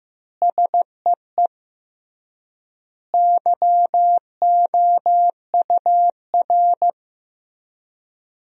Morse code, heard as a keyed tone, about 15 wpm, 700 Hz. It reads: SEE YOUR
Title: 3x Word-Spacing - Sentences from Top 500 Words - 15wpm